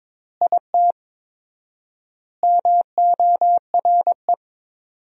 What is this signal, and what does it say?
Morse code, heard as a keyed tone, about 21 wpm, 700 Hz. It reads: IT MORE
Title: 4x Word-Spacing - Sentences from Top 400 Words - 22wpm